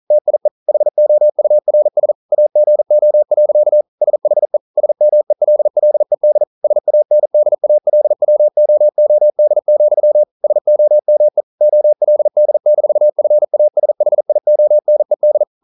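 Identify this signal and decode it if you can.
Morse code, heard as a keyed tone, about 42 wpm, 600 Hz. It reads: TIE HOURS AGO. SHE SMELLED SANDALWOOD, SOME OLD-FASHIONED
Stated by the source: Morse code practice at 41 wpm, from CW med Gnister.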